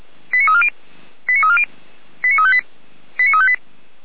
Five-tone selective call (ZVEI), heard as quick runs of five tones, tones of about 80 ms. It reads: 89340 89340 89358 89358
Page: ZVEI